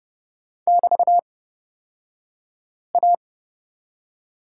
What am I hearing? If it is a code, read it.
Morse code, heard as a keyed tone, about 30 wpm, spaced as for 5 wpm, 700 Hz. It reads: <BT>A
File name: Lesson 36d - BT - Two Characters 30wpm